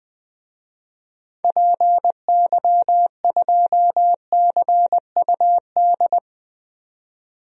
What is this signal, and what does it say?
Morse code, heard as a keyed tone, about 20 wpm, 700 Hz. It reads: PY2CUD
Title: DXCC-100 Call Signs - 6 Characters - 20wpm